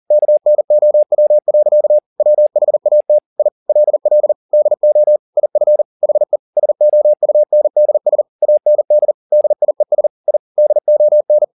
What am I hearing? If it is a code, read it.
Morse code, heard as a keyed tone, about 40 wpm, 600 Hz. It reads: KNOW. WHAT I LL DO IF HE SOUNDS AND DIES I DON